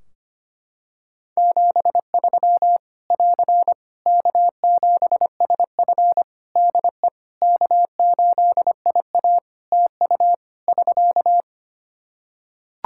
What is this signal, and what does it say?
Morse code, heard as a keyed tone, about 25 wpm, 700 Hz. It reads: 73 <AR> K7SF DE K8IA TU <SK>
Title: Practice QSO - Exchange 4 - Ending - 25wpm